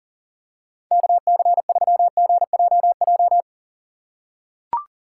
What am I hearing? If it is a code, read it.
Morse code, heard as a keyed tone, about 40 wpm, 700 Hz. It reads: KC3GJJ
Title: USA Call Signs - 6 Characters - 4x Character Spacing - 40wpm